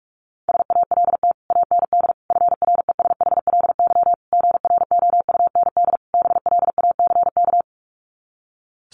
Morse code, heard as a keyed tone, about 44 wpm, 700 Hz. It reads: SALT AND FRESHLY GROUND BLACK
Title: Sets of 5 Words Encore 45wpm